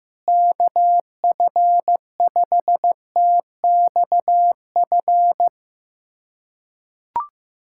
Morse code, heard as a keyed tone, about 15 wpm, 700 Hz. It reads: KF5TXF
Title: USA Callsign DE Callsign 15wpm